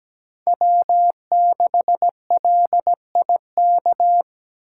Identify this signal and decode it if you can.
Morse code, heard as a keyed tone, about 17 wpm, 700 Hz. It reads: W6LIK
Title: USA Call Signs - 5 Characters - 3x Character Spacing - 17wpm